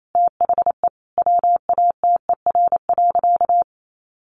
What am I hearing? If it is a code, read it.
Morse code, heard as a keyed tone, about 28 wpm, 700 Hz. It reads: THE WATER.
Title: Sentences from Top 500 Words 28wpm